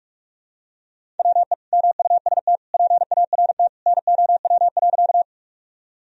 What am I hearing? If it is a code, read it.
Morse code, heard as a keyed tone, about 46 wpm, 700 Hz. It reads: WE MUST PART NOW.
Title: Sentences from Top 1000 Words - Review Part 1 - 45wpm